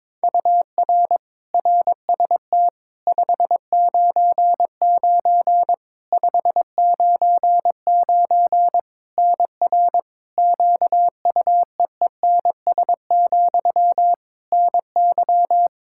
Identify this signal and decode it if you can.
Morse code, heard as a keyed tone, about 22 wpm, 700 Hz. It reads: UR RST 599 599 NR QUEENS, NY
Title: K1USN SST QSO Practice 22wpm